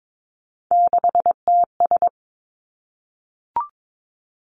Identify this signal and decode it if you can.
Morse code, heard as a keyed tone, about 22 wpm, 700 Hz. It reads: 6TS
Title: Three Challenging Letter-Number Combinations 22wpm